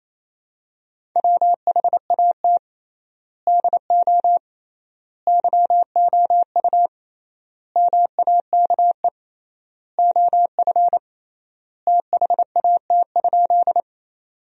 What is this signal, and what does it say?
Morse code, heard as a keyed tone, about 28 wpm, 700 Hz. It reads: WHAT DO YOU MAKE OF THAT?
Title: 3x Word-Spacing - Sentences from Top 100 Words - 28wpm